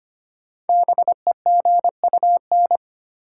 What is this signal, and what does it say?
Morse code, heard as a keyed tone, about 25 wpm, 700 Hz. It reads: BEGUN
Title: Sentences from Top 1000 Words - Review Part 6 - New Word Drill - 25wpm